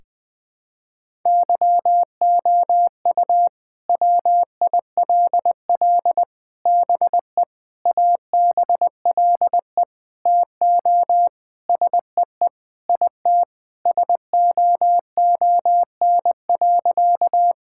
Morse code, heard as a keyed tone, about 21 wpm, 700 Hz. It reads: YOU WILL BE ABLE TO SEE IT SOON.
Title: Sentences with 4-Word Phrases - Spoken Beginning - 20wpm